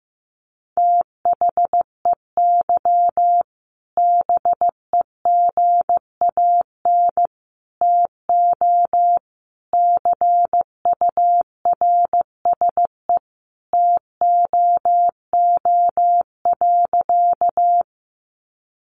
Morse code, heard as a keyed tone, about 15 wpm, 700 Hz. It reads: THEY BEGAN TO CURSE TOO.
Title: A Sound Of Thunder 15wpm